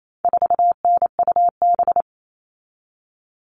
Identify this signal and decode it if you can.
Morse code, heard as a keyed tone, about 28 wpm, 700 Hz. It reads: 4NUB
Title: All Cut Numbers - 4-Digit 28wpm